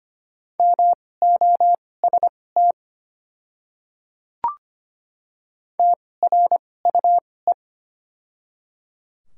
Morse code, heard as a keyed tone, about 25 wpm, spaced as for 18 wpm, 700 Hz. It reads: MOST TRUE
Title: 2x Character Spacing - Top 500 Words - 25wpm